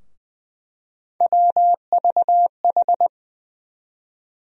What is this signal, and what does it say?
Morse code, heard as a keyed tone, about 20 wpm, 700 Hz. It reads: WVH
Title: USA Call Sign Postfixes - 3 Letters 20wpm